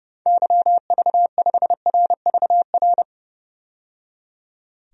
Morse code, heard as a keyed tone, about 29 wpm, 700 Hz. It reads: YV5RVR
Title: DXCC-100 Call Signs - 6 Characters - 30wpm